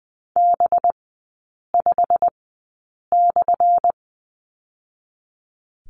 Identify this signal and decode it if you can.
Morse code, heard as a keyed tone, about 20 wpm, spaced as for 8 wpm, 700 Hz. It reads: B5/
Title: 184 Lesson 30e - Letter b - Three Characters 20wpm